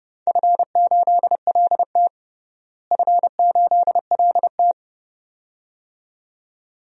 Morse code, heard as a keyed tone, about 32 wpm, 700 Hz. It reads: F8LT F8LT
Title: DXCC-100 Call Signs - 4 Characters - Repeated 2x - 30wpm